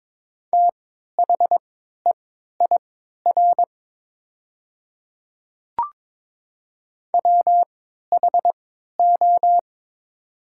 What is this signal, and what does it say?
Morse code, heard as a keyed tone, about 22 wpm, spaced as for 12 wpm, 700 Hz. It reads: THEIR WHO
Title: 3x Character Spacing - Top 100 Words - 22wpm